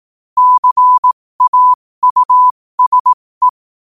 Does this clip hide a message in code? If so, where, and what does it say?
Yes: Morse code from 0.4 to 3.5 s, heard as a keyed tone, about 18 wpm, spaced as for 15 wpm, 1000 Hz. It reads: CAUSE